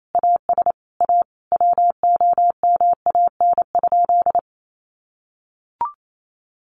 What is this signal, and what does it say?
Morse code, heard as a keyed tone, about 28 wpm, 700 Hz. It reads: AS A WOMAN?
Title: Sentences from Top 1100 Words 28wpm